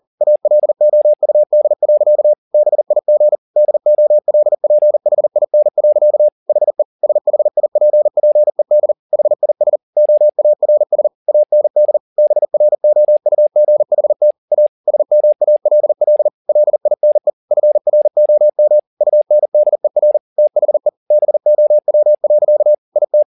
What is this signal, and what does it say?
Morse code, heard as a keyed tone, about 40 wpm, 600 Hz. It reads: ALOUD. BIG DOLPHIN. HE SHIPPED HIS OARS AND BROUGHT A SMALL LINE FROM UNDER THE BOW. IT